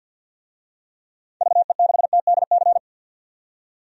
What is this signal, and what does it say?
Morse code, heard as a keyed tone, about 51 wpm, 700 Hz. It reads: VE6TDC
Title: DXCC-100 Call Signs - 6 Characters - 50wpm